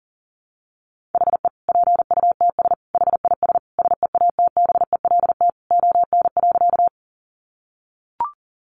Morse code, heard as a keyed tone, about 40 wpm, 700 Hz. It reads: HE PUTS HIS SEATBELT ON.
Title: Additional Easy Phrases 40wpm